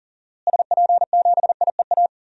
Morse code, heard as a keyed tone, about 40 wpm, 700 Hz. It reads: SP7IEA
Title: DXCC-100 Call Signs - 6 Characters - Repeated 3x - 40wpm